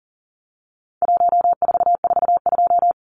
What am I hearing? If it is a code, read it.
Morse code, heard as a keyed tone, about 40 wpm, 700 Hz. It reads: 1442
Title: Numbers - 4-Digit 40wpm